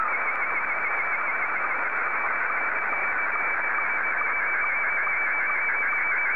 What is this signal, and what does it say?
SSTV (slow-scan television), Scottie 1.
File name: sstv / S1-ZL2CC